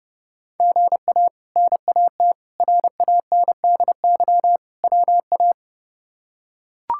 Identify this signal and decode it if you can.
Morse code, heard as a keyed tone, about 30 wpm, 700 Hz. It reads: GA NAT RANDY WA
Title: K1USN SST QSO Practice - 30wpm